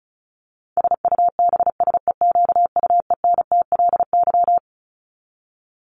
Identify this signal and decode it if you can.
Morse code, heard as a keyed tone, about 34 wpm, 700 Hz. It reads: SUBSEQUENTLY
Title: LY Suffix - Top 100 Words with Base Words 35wpm